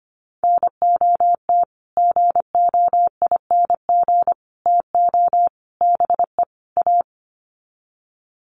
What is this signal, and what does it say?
Morse code, heard as a keyed tone, about 25 wpm, 700 Hz. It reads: NOT GOING TO BE A
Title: Sets of 5 Words Encore 25wpm